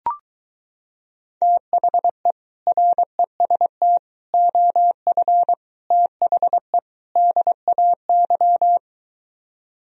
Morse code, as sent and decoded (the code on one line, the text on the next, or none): - .... / .-. ... - / --- ..-. / - .... / -.. .- -.--
THE REST OF THE DAY